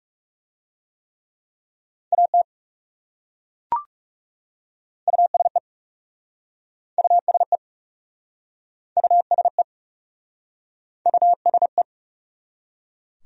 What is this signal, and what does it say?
Morse code, heard as a keyed tone, about 37 wpm, 700 Hz. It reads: AT USE USE USE USE
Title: Speed-Racing Top 100 Words 30wpm